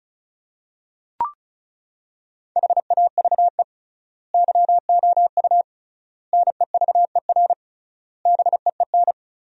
Morse code, heard as a keyed tone, about 36 wpm, 700 Hz. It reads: HAVE YOU NEVER BEEN
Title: 3x Word-Spacing - Sentences from Top 300 Words - 35wpm